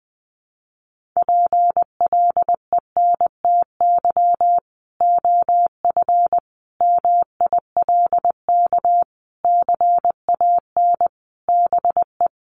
Morse code, heard as a keyed tone, about 20 wpm, 700 Hz. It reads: PLENTY OF MILK CAN BE
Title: Sentences from Top 1500 Words 20wpm